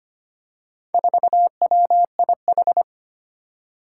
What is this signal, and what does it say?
Morse code, heard as a keyed tone, about 25 wpm, 700 Hz. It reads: 4WIH
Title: Four Challenging Letter-Number Combinations 25wpm